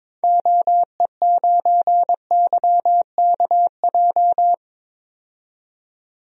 Morse code, heard as a keyed tone, about 22 wpm, 700 Hz. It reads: OE9YKJ
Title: DXCC-100 Call Signs - 6 Characters - 22wpm